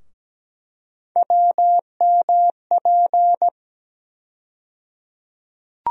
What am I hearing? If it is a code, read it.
Morse code, heard as a keyed tone, about 17 wpm, 700 Hz. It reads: WMP
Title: USA Call Sign Postfixes - 3 Letters 17wpm